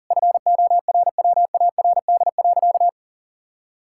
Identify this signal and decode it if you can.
Morse code, heard as a keyed tone, about 40 wpm, 700 Hz. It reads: FORWARD.
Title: Pangram Challenge v2 - 40wpm